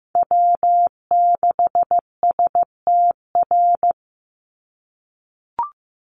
Morse code, heard as a keyed tone, about 15 wpm, 700 Hz. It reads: W6STR
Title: NA SOTA Chaser Callsigns 15wpm